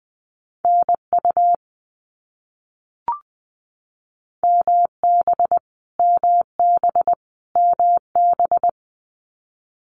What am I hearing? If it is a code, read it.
Morse code, heard as a keyed tone, about 20 wpm, 700 Hz. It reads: NU MB MB MB
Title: Canadian Provinces and Territories - Repeated 3x - 20wpm